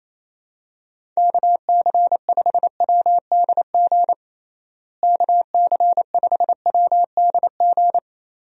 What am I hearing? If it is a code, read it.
Morse code, heard as a keyed tone, about 27 wpm, 700 Hz. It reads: KC5WDG KC5WDG
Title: USA Call Signs - 6 Characters - Repeated 3x - 28wpm